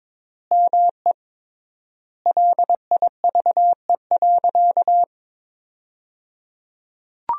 Morse code, heard as a keyed tone, about 22 wpm, 700 Hz. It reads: ME LIVE.
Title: 3x Word-Spacing - Sentences from Top 300 Words - 22wpm